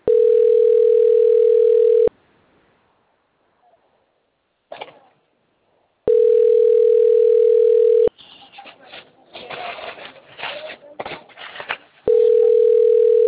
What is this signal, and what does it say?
Ringback tone (North America), 440 + 480 Hz.